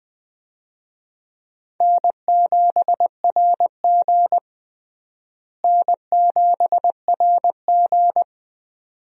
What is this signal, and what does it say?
Morse code, heard as a keyed tone, about 20 wpm, 700 Hz. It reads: N7RG N7RG
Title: USA Call Signs - 4 Characters - Repeated 2x - 20wpm